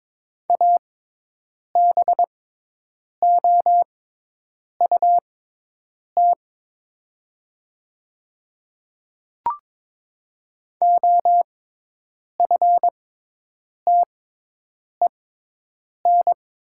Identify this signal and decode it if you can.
Morse code, heard as a keyed tone, about 22 wpm, spaced as for 8 wpm, 700 Hz. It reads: ABOUT OFTEN
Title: 6x Character Spacing - Top 300 Words - 22wpm